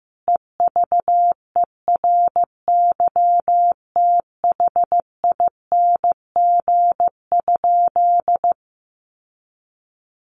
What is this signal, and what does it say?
Morse code, heard as a keyed tone, about 15 wpm, 700 Hz. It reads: EVERYTHING?
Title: Sentences from Top 1100 Words 15wpm